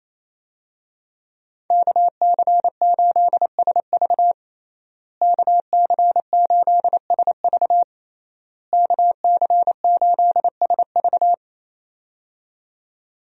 Morse code, heard as a keyed tone, about 27 wpm, 700 Hz. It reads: KC8SV KC8SV KC8SV
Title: USA Call Signs - 5 Characters - Repeated 3x - 28wpm